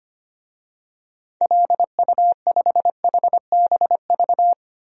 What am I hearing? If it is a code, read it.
Morse code, heard as a keyed tone, about 25 wpm, 700 Hz. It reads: LU5HBV
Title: DXCC-100 Call Signs - 6 Characters - Repeated 2x - 25wpm